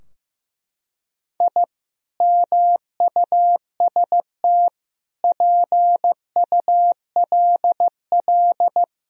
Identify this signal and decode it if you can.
Morse code, heard as a keyed tone, about 15 wpm, 700 Hz. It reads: I MUST PULL
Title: Sentences from Top 1000 Words - Review Part 2 - 15wpm